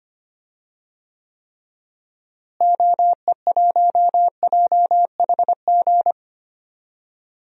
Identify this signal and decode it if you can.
Morse code, heard as a keyed tone, about 25 wpm, 700 Hz. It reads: OE1JHG
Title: DXCC-100 Call Signs - 6 Characters - Repeated 3x - 25wpm